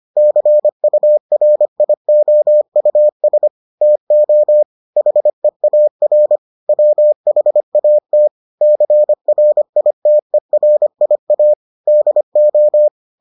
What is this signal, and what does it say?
Morse code, heard as a keyed tone, about 25 wpm, 600 Hz. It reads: CURIOUS TO HEAR WHAT CRITERIA DO